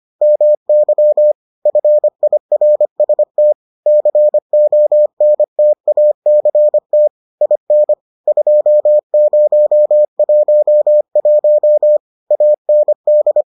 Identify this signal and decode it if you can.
Morse code, heard as a keyed tone, about 25 wpm, 600 Hz. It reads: MY FIRST CONTACT IN 2011 AND